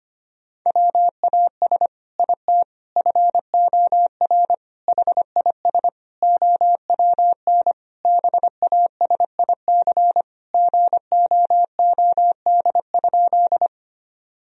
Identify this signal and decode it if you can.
Morse code, heard as a keyed tone, about 25 wpm, 700 Hz. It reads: WAS IT FOR HIS OWN BASIC GOOD?
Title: Sentences from Top 1000 Words - Review Part 3 - 25wpm